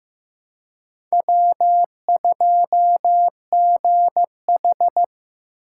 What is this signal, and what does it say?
Morse code, heard as a keyed tone, about 18 wpm, 700 Hz. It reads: W2GH
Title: Speed Racing - USA 1x2 Call Signs 15wpm